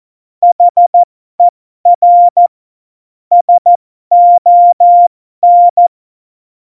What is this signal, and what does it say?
Morse code, heard as a keyed tone, about 14 wpm, spaced as for 12 wpm, 700 Hz. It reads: HER SON
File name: Morsekurs / Witze